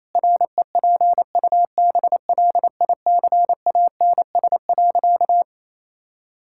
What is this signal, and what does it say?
Morse code, heard as a keyed tone, about 28 wpm, 700 Hz. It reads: REPUBLICANS.
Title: Pangram Challenge v2 - 28wpm